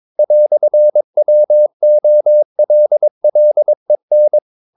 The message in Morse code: "WOLLEN

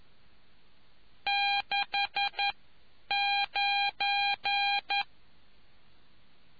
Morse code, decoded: 69

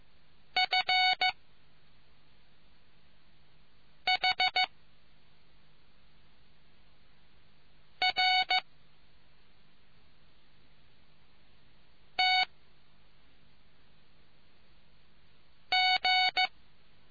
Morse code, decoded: F H R T G

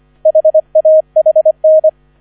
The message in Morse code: HAHN

